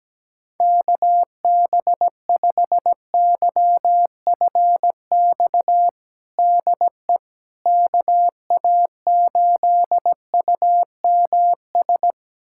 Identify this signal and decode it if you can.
Morse code, heard as a keyed tone, about 17 wpm, 700 Hz. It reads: KB5YFX DE KA8UMS
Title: USA Callsign DE Callsign - 6 Characters - 17wpm